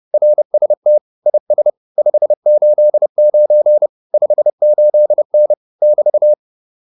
Morse code, heard as a keyed tone, about 30 wpm, 600 Hz. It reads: RST IS 589 58N <BT>